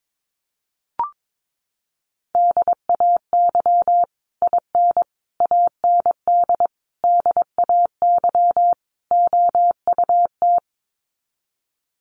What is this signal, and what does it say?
Morse code, heard as a keyed tone, about 22 wpm, 700 Hz. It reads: DAY IN AND DAY OUT
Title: Sets of 5 Words Encore2 22wpm